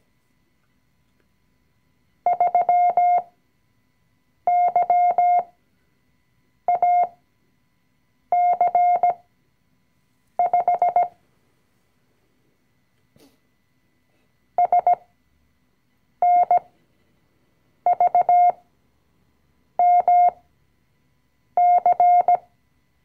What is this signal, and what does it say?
Morse code, heard as a keyed tone, about 17 wpm, spaced as for 6 wpm, 710 Hz. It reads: 3YAC5 SNVMC
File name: CW-Online-Kurs / Anfänger / Lektion 020